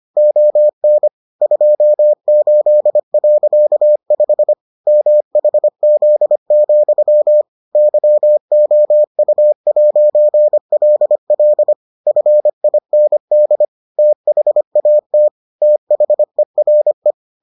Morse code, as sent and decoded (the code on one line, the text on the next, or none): --- -. / ..--- ---.. .-.-.- ..... / -- .... --.. --..-- / -.-- --- ..- .----. .-.. .-.. / ..-. .. -. -.. / - .... .- - / - .... .-. .
ON 28.5 MHZ, YOU'LL FIND THAT THERE